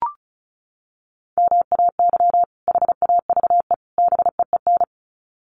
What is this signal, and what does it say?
Morse code, heard as a keyed tone, about 34 wpm, 700 Hz. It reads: MAY HAVE BEEN
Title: Sets of 3 Words 35wpm